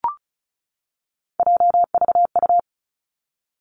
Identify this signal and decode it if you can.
Morse code, heard as a keyed tone, about 36 wpm, 700 Hz. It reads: JVU
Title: Three Challenging Letter-Number Combinations 35wpm